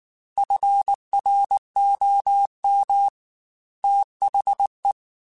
Morse code, heard as a keyed tone, about 19 wpm, 800 Hz. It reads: FROM THE
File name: Weather Sets / Weather Sets-mp3s / Weather 5 12 @18